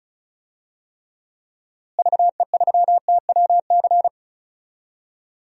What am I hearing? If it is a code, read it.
Morse code, heard as a keyed tone, about 36 wpm, 700 Hz. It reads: VE3TWC
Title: DXCC-100 Call Signs - 6 Characters - Repeated 3x - 35wpm